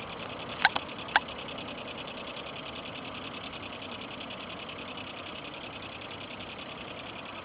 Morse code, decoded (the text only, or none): none